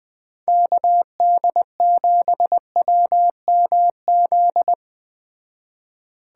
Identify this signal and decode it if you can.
Morse code, heard as a keyed tone, about 20 wpm, 700 Hz. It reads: KD7WMZ